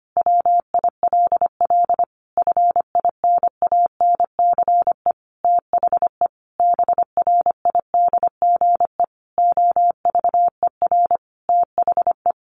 Morse code, heard as a keyed tone, about 25 wpm, 700 Hz. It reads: WILL FINANCE THE BRIDGE OVER THE
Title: Sentences from Top 1500 Words 25wpm